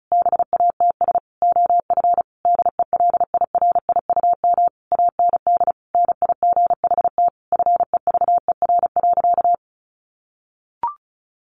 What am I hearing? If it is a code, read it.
Morse code, heard as a keyed tone, about 34 wpm, 700 Hz. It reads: BATS OF DELIRIUM AND NIGHT FEVER.
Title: A Sound Of Thunder 35wpm